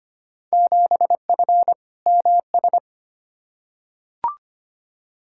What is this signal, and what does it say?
Morse code, heard as a keyed tone, about 25 wpm, 700 Hz. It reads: 7F MS
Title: Field Day Exchanges Encore - 25wpm